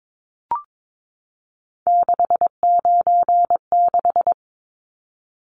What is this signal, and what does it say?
Morse code, heard as a keyed tone, about 22 wpm, 700 Hz. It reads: 696
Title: Numbers - 3-Digit 22wpm